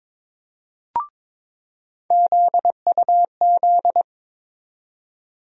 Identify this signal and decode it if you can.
Morse code, heard as a keyed tone, about 21 wpm, 700 Hz. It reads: ZUZ
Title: USA Call Sign Postfixes - 3 Letters 22wpm